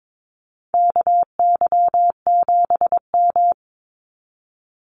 Morse code, heard as a keyed tone, about 22 wpm, 700 Hz. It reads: KY7M
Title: Call Signs Repeated 3x - 22wpm